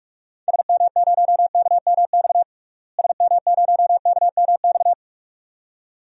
Morse code, heard as a keyed tone, about 46 wpm, 700 Hz. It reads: SM0KMX SM0KMX
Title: DXCC-100 Call Signs - 6 Characters - Repeated 2x - 45wpm